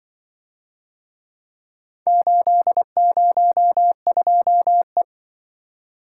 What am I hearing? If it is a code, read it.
Morse code, heard as a keyed tone, about 24 wpm, 700 Hz. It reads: 802E
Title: Speed-Racing - Four Letter-Number Combinations - 20wpm